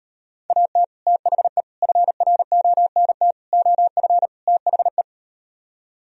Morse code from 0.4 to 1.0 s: AT

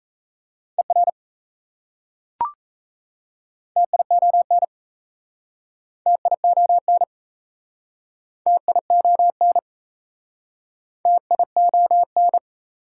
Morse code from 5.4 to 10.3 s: TION TION

